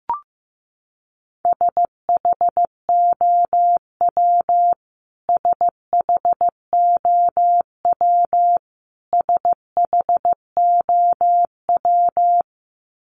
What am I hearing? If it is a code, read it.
Morse code, heard as a keyed tone, about 15 wpm, 700 Hz. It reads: SHOW SHOW SHOW
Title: Top 500 Words Repeated 15wpm